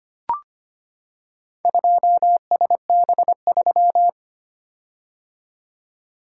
Morse code, heard as a keyed tone, about 25 wpm, 700 Hz. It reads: 2SB3
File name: practice / 25wpm / Four Letter-Number Combinations - 25wpm